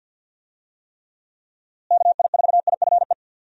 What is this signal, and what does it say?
Morse code, heard as a keyed tone, about 50 wpm, 700 Hz. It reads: KI4IFE
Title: USA Call Signs - 6 Characters - Repeated 3x - 50wpm